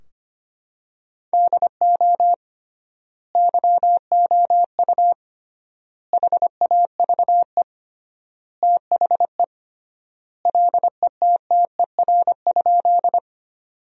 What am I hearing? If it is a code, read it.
Morse code, heard as a keyed tone, about 25 wpm, 700 Hz. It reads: DO YOU HAVE THE LETTER?
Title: 3x Word-Spacing - Sentences from Top 300 Words - 25wpm